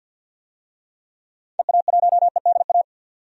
Morse code, heard as a keyed tone, about 52 wpm, 700 Hz. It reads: EA1EDA
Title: DXCC-100 Call Signs - 6 Characters - 50wpm